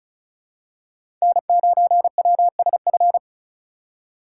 Morse code, heard as a keyed tone, about 36 wpm, 700 Hz. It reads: N9WSF
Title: USA Call Signs - 8x Character Spacing - 35wpm